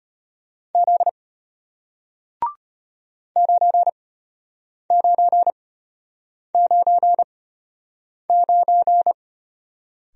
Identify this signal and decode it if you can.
Morse code, heard as a keyed tone, about 34 wpm, spaced as for 8 wpm, 700 Hz. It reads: Z 9999